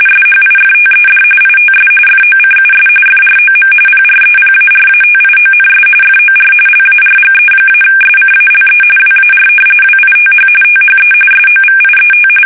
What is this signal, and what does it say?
RTTY, about 45.45 baud. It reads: SHIFTED ITS FOCUS FROM PREPARATIONS FOR LARGE-SCALE WAR WITH THE SOVIET
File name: fsk850 / usnavy5